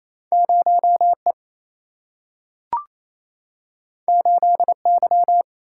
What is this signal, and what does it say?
Morse code, heard as a keyed tone, about 28 wpm, 700 Hz. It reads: 0E 8Y
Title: Single-Letter Single-Number 28wpm